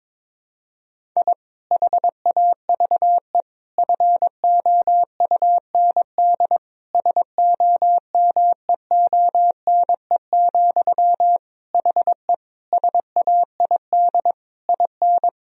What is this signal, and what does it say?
Morse code, heard as a keyed tone, about 22 wpm, 700 Hz. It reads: I HAVE FOUND SOMEONE, HE SAID IN
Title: Sentences from Top 1000 Words - Review Part 1 - 22wpm